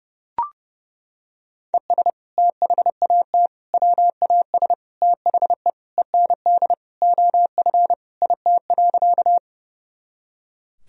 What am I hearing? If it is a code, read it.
Morse code, heard as a keyed tone, about 30 wpm, 700 Hz. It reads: ES THAT WAS THE END OF IT.
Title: Sentences from Top 200 Words and Abbreviations 30wpm